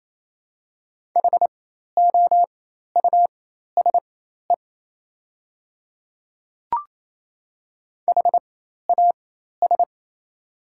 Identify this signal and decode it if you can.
Morse code, heard as a keyed tone, about 27 wpm, spaced as for 13 wpm, 690 Hz. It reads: HOUSE HAS
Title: 4x Character Spacing - Top 200 Words - 28wpm